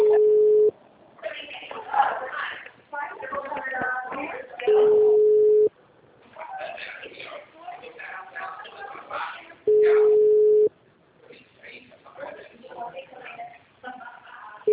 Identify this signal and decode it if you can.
Ringback tone (Europe, 425 Hz), with speech.
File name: out- -2016- - - .96035